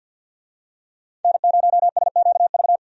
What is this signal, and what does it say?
Morse code, heard as a keyed tone, about 52 wpm, 700 Hz. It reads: N0SQV
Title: USA Call Signs - 5 Characters - 50wpm